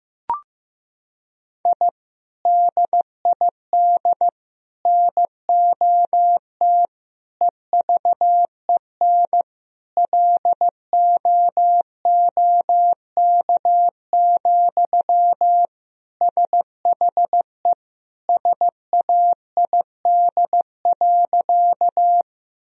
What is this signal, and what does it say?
Morse code, heard as a keyed tone, about 15 wpm, 700 Hz. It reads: I DID NOT EVEN LOOK, SHE SAID.
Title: Sentences from Top 1000 Words - Review Part 1 - 15wpm